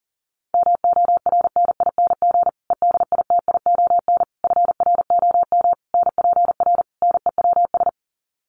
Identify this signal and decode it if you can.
Morse code, heard as a keyed tone, about 40 wpm, 700 Hz. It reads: MORNING EDITION FROM NPR NEWS
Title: Sets of 5 Words Encore 40wpm